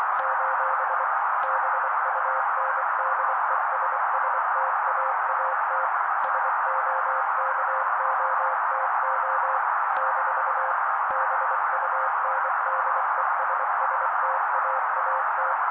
Morse code, heard as a keyed tone, about 24 wpm, 530 Hz. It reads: : BUNDESSTAAT SOKOTO - BUNDESSTAAT
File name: daily / 23wpm noise